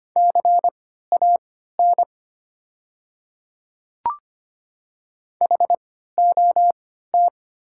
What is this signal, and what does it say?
Morse code, heard as a keyed tone, about 25 wpm, spaced as for 14 wpm, 700 Hz. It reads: CAN HOT
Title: 3x Character Spacing - Top 200 Words - 25wpm